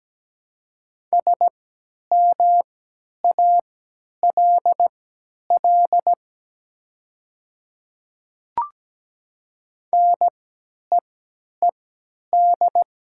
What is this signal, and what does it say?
Morse code, heard as a keyed tone, about 17 wpm, spaced as for 10 wpm, 700 Hz. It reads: SMALL NEED
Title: 3x Character Spacing - Top 200 Words - 17wpm